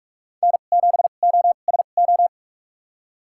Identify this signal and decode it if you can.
Morse code, heard as a keyed tone, about 43 wpm, spaced as for 32 wpm, 700 Hz. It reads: N7OSO
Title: USA Call Signs - 5 Characters - 2x Character Spacing - 45wpm